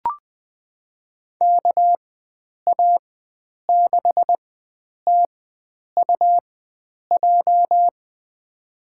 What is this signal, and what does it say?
Morse code, heard as a keyed tone, about 21 wpm, spaced as for 9 wpm, 700 Hz. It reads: KA6TUJ
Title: USA Call Signs - 6 Characters - 4x Character Spacing - 20wpm